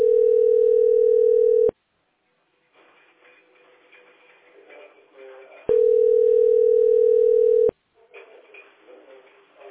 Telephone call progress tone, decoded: ringback tone

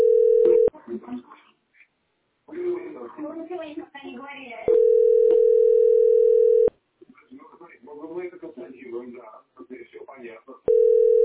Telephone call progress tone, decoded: ringback tone

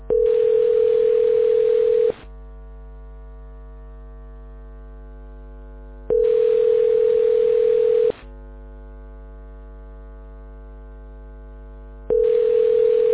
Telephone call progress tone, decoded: ringback tone

